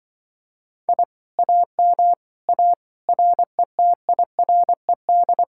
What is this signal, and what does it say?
Morse code, heard as a keyed tone, about 24 wpm, 700 Hz. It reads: I AM A RETIRED
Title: Speed-Racing Common QSO Phrases Encore - 20wpm